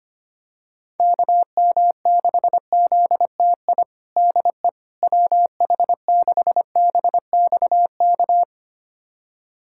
Morse code, heard as a keyed tone, about 25 wpm, 700 Hz. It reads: KM6ZTI DE WH6BXK
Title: USA Callsign DE Callsign - 6 Characters - 25wpm